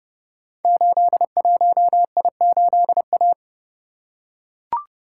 Morse code, heard as a keyed tone, about 32 wpm, 700 Hz. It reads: 81I8A
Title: Code Groups - 5 Letter:Numbers - 30wpm